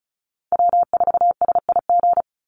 Morse code, heard as a keyed tone, about 34 wpm, 700 Hz. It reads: W4SIG
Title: Call Signs - 35wpm